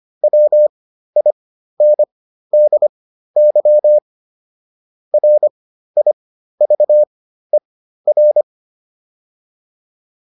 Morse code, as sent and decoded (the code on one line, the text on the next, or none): .-- .. -. -.. -.-- / .-. .. ...- . .-.
WINDY RIVER